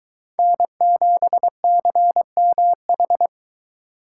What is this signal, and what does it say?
Morse code, heard as a keyed tone, about 23 wpm, 700 Hz. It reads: N7CMH